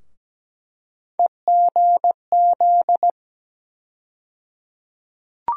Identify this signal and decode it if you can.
Morse code, heard as a keyed tone, about 17 wpm, 700 Hz. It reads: EGZ
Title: USA Call Sign Postfixes - 3 Letters 17wpm